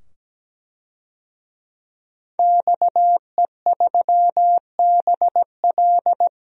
Morse code, heard as a keyed tone, about 17 wpm, 700 Hz. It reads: XE3BL
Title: DXCC-100 Call Signs - 5 Characters - 17wpm